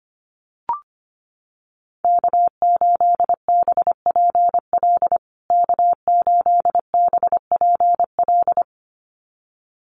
Morse code, heard as a keyed tone, about 25 wpm, 700 Hz. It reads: K8BPL K8BPL
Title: USA Call Signs Repeated 2x 25wpm